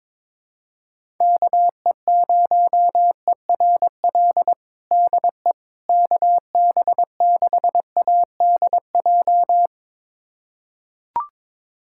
Morse code, heard as a keyed tone, about 21 wpm, 700 Hz. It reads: KE0ERL DE KB6ADJ
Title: USA Callsign DE Callsign 22wpm